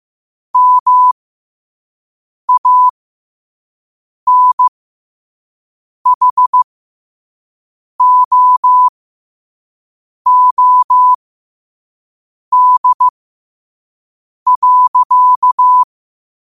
Morse code, heard as a keyed tone, about 15 wpm, spaced as for 5 wpm, 1000 Hz. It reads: MANHOOD.
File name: qotdfiles / 5wpm